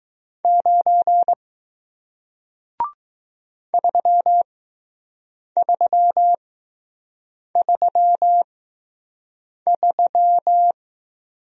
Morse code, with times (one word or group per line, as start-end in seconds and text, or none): 0.4-1.3 9
3.7-10.7 3333